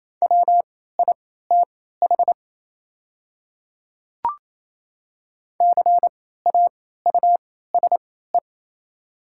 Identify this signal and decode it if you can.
Morse code, heard as a keyed tone, about 27 wpm, spaced as for 16 wpm, 700 Hz. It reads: WITH CAUSE
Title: 3x Character Spacing - Top 200 Words - 28wpm